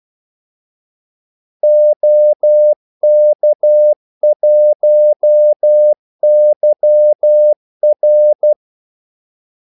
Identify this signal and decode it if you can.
Morse code, heard as a keyed tone, about 12 wpm, 600 Hz. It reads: OK1YR
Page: 2 ekstra ordmellemrum mellem kaldesignalerne